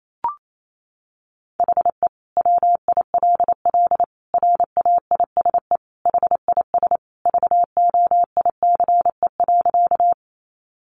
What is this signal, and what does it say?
Morse code, heard as a keyed tone, about 28 wpm, 700 Hz. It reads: HE WILL RAISE HIS VOICE.
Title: Sentences from Top 1400 Words 28wpm